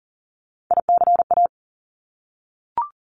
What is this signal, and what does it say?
Morse code, heard as a keyed tone, about 40 wpm, 700 Hz. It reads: ICA
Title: Top Three Letter Combinations in Words 40wpm